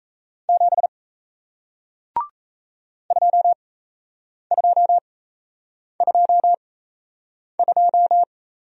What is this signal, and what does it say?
Morse code, heard as a keyed tone, about 37 wpm, spaced as for 8 wpm, 700 Hz. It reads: 7 2222